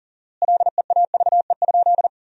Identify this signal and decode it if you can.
Morse code, heard as a keyed tone, about 37 wpm, 700 Hz. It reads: LEAVE?
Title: 4x Word-Spacing - Sentences from Top 400 Words - 40wpm